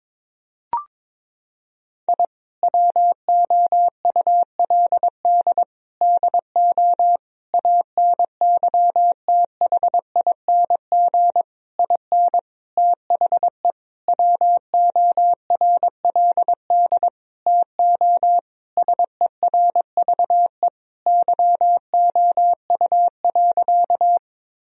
Morse code, heard as a keyed tone, about 22 wpm, 700 Hz. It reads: I WOULD DO ANYTHING IN THE WORLD TO SERVE YOU.
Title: Sentences from Top 1000 Words - Review Part 2 - 22wpm